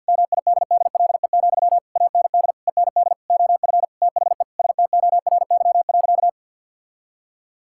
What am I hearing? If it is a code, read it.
Morse code, heard as a keyed tone, about 50 wpm, 700 Hz. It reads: MIDDLE, AND END OF THE STORY.
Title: Sentences from Top 1000 Words - Review Part 6 - 50wpm